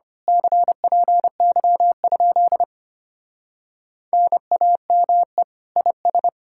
Morse code, heard as a keyed tone, about 24 wpm, 700 Hz. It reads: CPY? NAME IS
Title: Speed-Racing Common QSO Phrases Encore - 25wpm